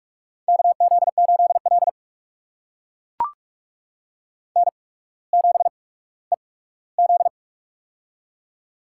Morse code, heard as a keyed tone, about 44 wpm, spaced as for 19 wpm, 700 Hz. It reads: KZ8L N7EZ
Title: USA Call Signs - 4 Characters - 8x Character Spacing - 45wpm